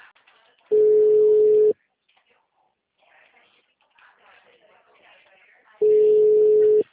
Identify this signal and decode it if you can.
Ringback tone (Europe, 425 Hz), with speech.
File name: out- -2022- - - .70876